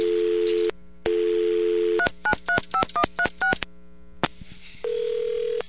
DTMF, heard as telephone touch tones, tones of about 80 ms. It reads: 3935136